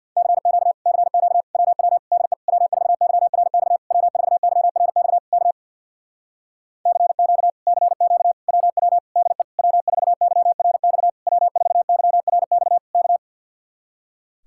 Morse code, heard as a keyed tone, about 63 wpm, 700 Hz. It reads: CQ CQ WW DE W4YRX W4YRX K CQ CQ WW DE W4YRX W4YRX K